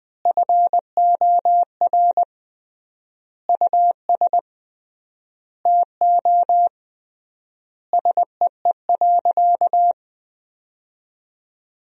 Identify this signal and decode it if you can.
Morse code, heard as a keyed tone, about 21 wpm, 700 Hz. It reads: FOR US TO SEE.
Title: 3x Word-Spacing - Sentences from Top 500 Words - 20wpm